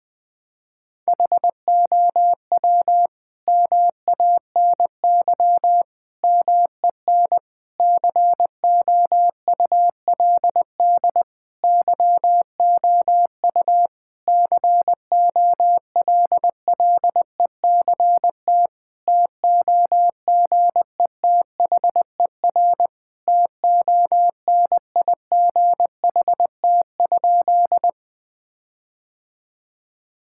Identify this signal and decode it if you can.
Morse code, heard as a keyed tone, about 20 wpm, 700 Hz. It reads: HOW MANY MEN COULD YOU COLLECT TOGETHER TONIGHT?
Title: Sentences from Top 1600 Words 20wpm